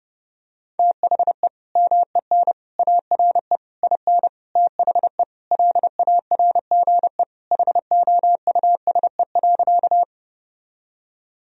Morse code, heard as a keyed tone, about 30 wpm, 700 Hz. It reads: THE MEN ARE IN THE LARGE HOUSE.
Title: Sentences from Top 1000 Words - Review Part 1 - 30wpm